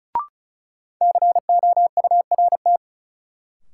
Morse code, heard as a keyed tone, about 36 wpm, 700 Hz. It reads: COURT
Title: Sentences from Top 1600 Words - New Word Drill Alt - 35wpm